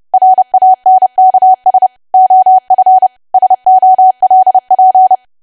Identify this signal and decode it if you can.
Morse code, heard as a keyed tone, about 30 wpm, 750 Hz. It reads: RANKS OF SOLP
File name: cwa / W1AW-Level3 / 30WPM